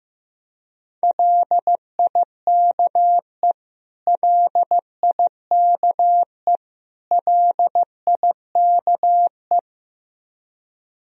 Morse code, heard as a keyed tone, about 15 wpm, 700 Hz. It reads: LIKE LIKE LIKE
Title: Top 100 Words Repeated 15wpm